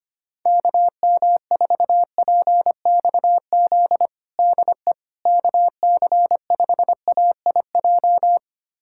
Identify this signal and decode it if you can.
Morse code, heard as a keyed tone, about 25 wpm, 700 Hz. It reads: KM4PXZ DE KC5AIJ
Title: USA Callsign DE Callsign - 6 Characters - 25wpm